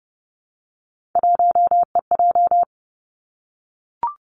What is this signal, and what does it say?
Morse code, heard as a keyed tone, about 30 wpm, 700 Hz. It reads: 1EJ
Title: Three Challenging Letter-Number Combinations 30wpm